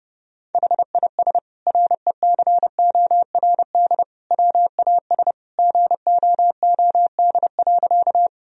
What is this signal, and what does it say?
Morse code, heard as a keyed tone, about 30 wpm, 700 Hz. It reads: HIS RECORD WAS GOOD.
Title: Sentences from Top 1000 Words - Review Part 3 - 30wpm